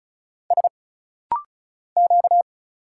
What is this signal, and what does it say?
Morse code, heard as a keyed tone, about 36 wpm, 700 Hz. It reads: S Q
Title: Single Letters - Rapid-Fire 35wpm